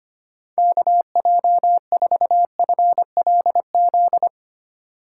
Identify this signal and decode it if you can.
Morse code, heard as a keyed tone, about 25 wpm, 700 Hz. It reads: KJ4FLZ
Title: USA Call Signs - 6 Characters - 3x Character Spacing - 25wpm